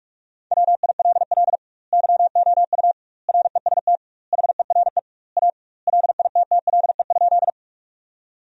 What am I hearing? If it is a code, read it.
Morse code, heard as a keyed tone, about 46 wpm, 700 Hz. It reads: WILL YOU REST HERE A LITTLE?
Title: 2x Word-Spacing - Sentences from Top 500 Words - 45wpm